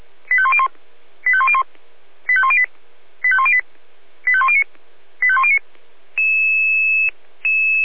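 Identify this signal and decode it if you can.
Five-tone selective call (ZVEI), heard as quick runs of five tones, tones of about 70 ms. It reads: 85101 85101 85108 85108 85109 85109
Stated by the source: ZVEI five-tone selective call received by SDR.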